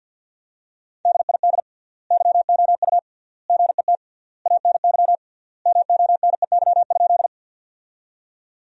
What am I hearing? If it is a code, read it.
Morse code, heard as a keyed tone, about 50 wpm, 700 Hz. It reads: DID YOU GET ANY MONEY?
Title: 3x Word-Spacing - Sentences from Top 400 Words - 50wpm